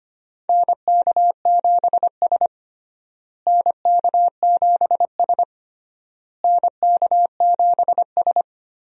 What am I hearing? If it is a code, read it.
Morse code, heard as a keyed tone, about 25 wpm, 700 Hz. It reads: NK7S NK7S NK7S
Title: USA Call Signs - 4 Characters - Repeated 3x - 25wpm